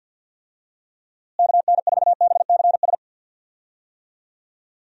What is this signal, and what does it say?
Morse code, heard as a keyed tone, about 52 wpm, 700 Hz. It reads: KN4BCS
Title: USA Call Signs - 6 Characters - Repeated 2x - 50wpm